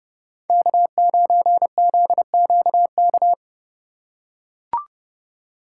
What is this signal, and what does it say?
Morse code, heard as a keyed tone, about 29 wpm, 700 Hz. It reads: K9ZQK